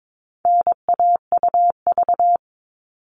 Morse code, heard as a keyed tone, about 22 wpm, 700 Hz. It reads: NAUV
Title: All Cut Numbers - 4-Digit 22wpm